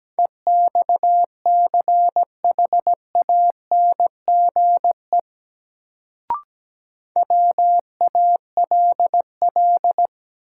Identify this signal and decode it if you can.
Morse code, heard as a keyed tone, about 17 wpm, 700 Hz. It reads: EXCHANGE WALL
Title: Sentences from Top 1000 Words - Review Part 6 - New Word Drill - 17wpm